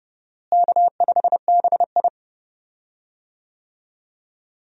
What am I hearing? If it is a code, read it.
Morse code, heard as a keyed tone, about 30 wpm, 700 Hz. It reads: K5BI
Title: USA Call Signs - 4 Characters - Repeated 2x - 30wpm